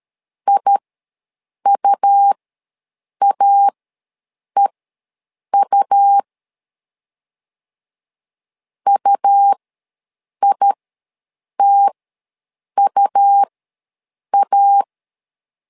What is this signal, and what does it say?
Morse code, heard as a keyed tone, about 13 wpm, spaced as for 7 wpm, 800 Hz. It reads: IUAEU UITUA
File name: LEZIONI / 2 UHDN / 2 U ws 12wpm 1655